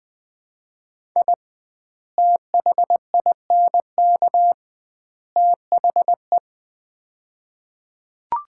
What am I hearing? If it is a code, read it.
Morse code, heard as a keyed tone, about 20 wpm, 700 Hz. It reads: I THINK THE
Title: 2x Word-Spacing - Sets of 3 Words - 20wpm